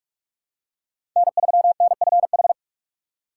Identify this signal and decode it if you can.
Morse code, heard as a keyed tone, about 43 wpm, 700 Hz. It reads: N 3 N F H
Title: USA Call Signs - 5 Characters - 6x Character Spacing - 45wpm